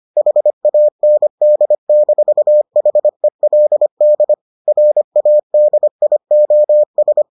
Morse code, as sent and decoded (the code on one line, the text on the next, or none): .... .- -. -.. -....- .... .-.. -.. / .-. .- -.. .. --- ...
HAND-HELD RADIOS